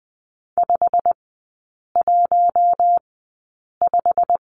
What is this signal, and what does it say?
Morse code, heard as a keyed tone, about 20 wpm, spaced as for 8 wpm, 700 Hz. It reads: 515
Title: 093 Lesson 16e - Letter c - Three Characters 20wpm